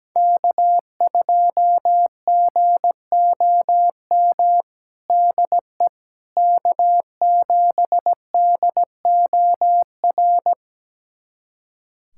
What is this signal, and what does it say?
Morse code, heard as a keyed tone, about 17 wpm, 700 Hz. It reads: K2GOM DE K7DOR
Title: USA Callsign DE Callsign - 5 Characters - 17wpm